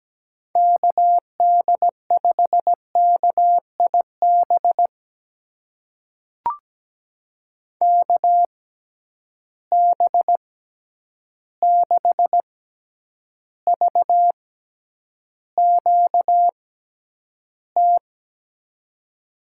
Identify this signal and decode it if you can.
Morse code, heard as a keyed tone, about 17 wpm, spaced as for 9 wpm, 700 Hz. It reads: KD5KIB KB6VQT
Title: USA Call Signs - 6 Characters - 6x Character Spacing - 17wpm